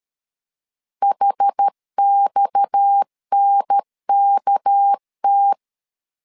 Morse code, heard as a keyed tone, about 13 wpm, 800 Hz. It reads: HXNKT
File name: LEZIONI / 6 PKX0 / 6 PKX 12wpm 2024